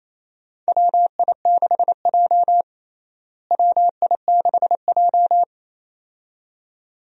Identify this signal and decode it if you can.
Morse code, heard as a keyed tone, about 27 wpm, 700 Hz. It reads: WI6J WI6J
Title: USA Call Signs - 4 Characters - Repeated 2x - 28wpm